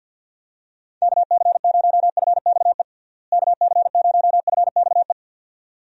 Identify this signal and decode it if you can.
Morse code, heard as a keyed tone, about 51 wpm, 700 Hz. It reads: KC0FXE KC0FXE
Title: USA Call Signs - 6 Characters - Repeated 2x - 50wpm